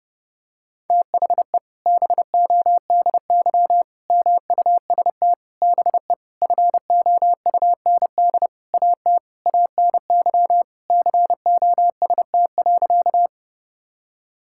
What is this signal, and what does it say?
Morse code, heard as a keyed tone, about 31 wpm, 700 Hz. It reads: THE BODY MUST BE FOUND AT ANY COST.
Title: Sentences from Top 1000 Words - Review Part 5 - 30wpm